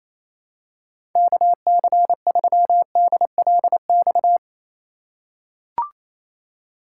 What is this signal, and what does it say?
Morse code, heard as a keyed tone, about 28 wpm, 700 Hz. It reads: KC3DLX
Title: USA Call Signs - 6 Characters - 4x Character Spacing - 28wpm